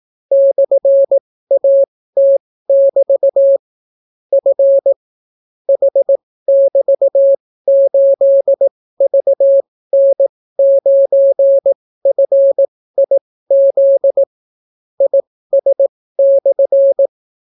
Morse code, heard as a keyed tone, about 18 wpm, spaced as for 14 wpm, 550 Hz. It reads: /AT<BT> F H<BT>8VN9FIZ IS/